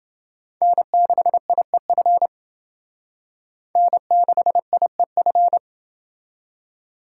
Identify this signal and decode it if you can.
Morse code, heard as a keyed tone, about 28 wpm, 700 Hz. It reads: N6IEF N6IEF